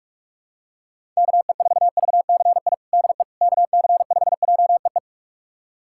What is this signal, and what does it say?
Morse code, heard as a keyed tone, about 42 wpm, 700 Hz. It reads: KE4VCI DE KC5JEE
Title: USA Callsign DE Callsign - 6 Characters - 45wpm